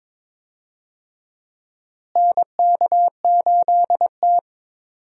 Morse code, heard as a keyed tone, about 21 wpm, 700 Hz. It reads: NK8T
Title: USA Call Signs - 4 Characters - Repeated 3x - 22wpm